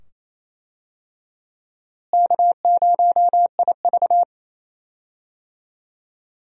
Morse code, heard as a keyed tone, about 28 wpm, 700 Hz. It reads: K0IV
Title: USA Call Signs - 4 Characters - Repeated 2x - 28wpm